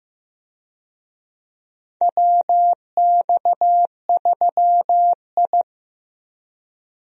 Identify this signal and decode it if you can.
Morse code, heard as a keyed tone, about 15 wpm, 700 Hz. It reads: WX3I
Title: USA Call Signs - 4 Characters - 15wpm